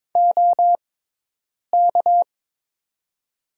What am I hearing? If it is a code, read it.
Morse code, heard as a keyed tone, about 22 wpm, spaced as for 8 wpm, 700 Hz. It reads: OK
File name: Top 90 QSO Elements - 6x Character Spacing - 22wpm